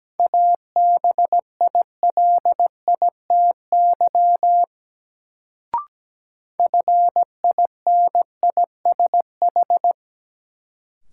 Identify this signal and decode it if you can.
Morse code, heard as a keyed tone, about 17 wpm, 700 Hz. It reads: ABILITY FINISH
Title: Sentences from Top 1200 Words - New Word Drill Alt - 17wpm